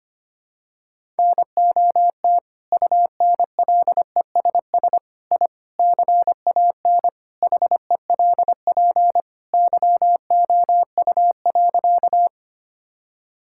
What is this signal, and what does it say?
Morse code, heard as a keyed tone, about 25 wpm, 700 Hz. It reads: NOT UNLESS I CAN HELP YOU.
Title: Sentences from Top 1000 Words - Review Part 6 - 25wpm